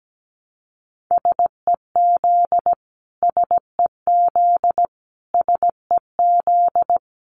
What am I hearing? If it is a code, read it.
Morse code, heard as a keyed tone, about 17 wpm, 700 Hz. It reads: SEZ SEZ SEZ
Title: Abbreviations Repeated 3x - 17wpm